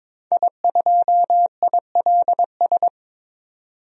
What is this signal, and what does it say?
Morse code, heard as a keyed tone, about 22 wpm, 700 Hz. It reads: I2ILS
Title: DXCC-100 Call Signs - 5 Characters - 22wpm